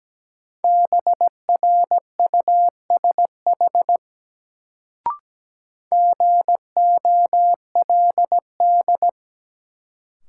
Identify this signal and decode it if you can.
Morse code, heard as a keyed tone, about 17 wpm, 700 Hz. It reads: BRUSH GOLD
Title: Sentences from Top 1100 Words - New Word Drill Alt - 17wpm